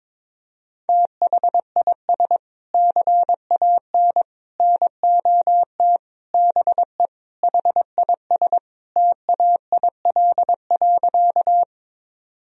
Morse code, heard as a keyed tone, about 22 wpm, 700 Hz. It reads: THIS CAN NOT BE HIS TAIL.
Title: Sentences from Top 1000 Words - Review Part 3 - 22wpm